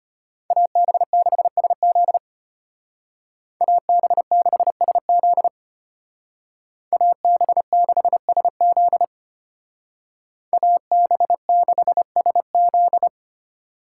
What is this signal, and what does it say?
Morse code, heard as a keyed tone, about 33 wpm, 700 Hz. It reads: AB6SZ AB6SZ AB6SZ AB6SZ